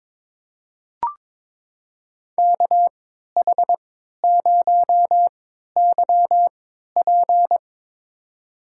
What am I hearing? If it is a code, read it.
Morse code, heard as a keyed tone, about 22 wpm, spaced as for 12 wpm, 700 Hz. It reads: KH0YP
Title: USA Call Signs - 3x Character Spacing - 22wpm